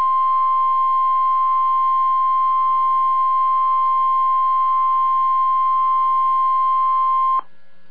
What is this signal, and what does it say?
Emergency Alert System (SAME) tones (the 1050 Hz attention tone). It is heard from 1.0 to 7.1 s.